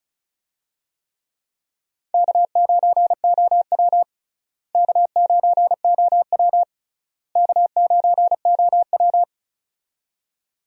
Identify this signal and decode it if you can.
Morse code, heard as a keyed tone, about 34 wpm, 700 Hz. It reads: K9OW K9OW K9OW
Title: USA Call Signs - 4 Characters - Repeated 3x - 35wpm